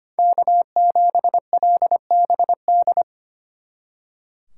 Morse code, heard as a keyed tone, about 33 wpm, 700 Hz. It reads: K7LBD